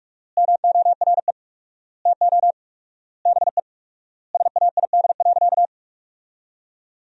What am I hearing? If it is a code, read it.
Morse code, heard as a keyed tone, about 41 wpm, 700 Hz. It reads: MORE TO BE SAID.
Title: 4x Word-Spacing - Sentences from Top 100 Words - 45wpm